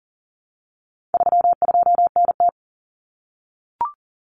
Morse code, heard as a keyed tone, about 40 wpm, 700 Hz. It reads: 32NT
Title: Common Cut Numbers - 4-Digit 40wpm